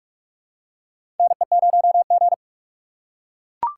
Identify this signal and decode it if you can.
Morse code, heard as a keyed tone, about 44 wpm, 690 Hz. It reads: NE0G
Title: USA Call Signs - 4 Characters - 2x Character Spacing - 45wpm